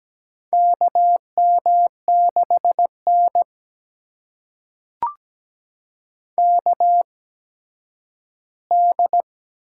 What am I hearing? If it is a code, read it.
Morse code, heard as a keyed tone, about 17 wpm, spaced as for 7 wpm, 700 Hz. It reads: KM6NKD
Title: USA Call Signs - 8x Character Spacing - 17wpm